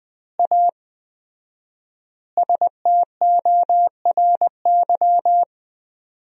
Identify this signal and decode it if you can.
Morse code, heard as a keyed tone, about 20 wpm, 700 Hz. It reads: A STORY
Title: 4x Word-Spacing - Sentences from Top 500 Words - 20wpm